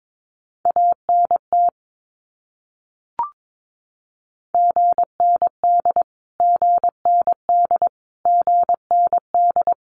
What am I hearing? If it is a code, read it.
Morse code, heard as a keyed tone, about 22 wpm, 700 Hz. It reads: ANT GND GND GND
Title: Abbreviations Repeated 3x - 22wpm